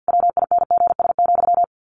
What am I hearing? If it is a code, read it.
Morse code, heard as a keyed tone, about 50 wpm, 700 Hz. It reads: WINGS,
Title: A Sound Of Thunder 50wpm